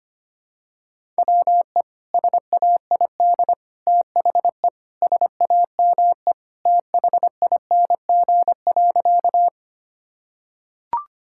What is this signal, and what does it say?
Morse code, heard as a keyed tone, about 25 wpm, 700 Hz. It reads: WE SAID THE SAME THING.
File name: practice / 25wpm / Sentences with 3-Word Phrases - 25wpm